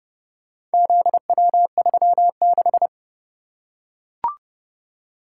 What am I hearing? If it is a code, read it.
Morse code, heard as a keyed tone, about 29 wpm, 700 Hz. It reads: ZW36